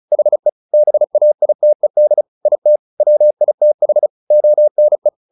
Morse code, heard as a keyed tone, about 34 wpm, 600 Hz. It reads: HE BAITED IT WITH ONE
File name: the old man and the sea / the old man and the sea 1 35wpm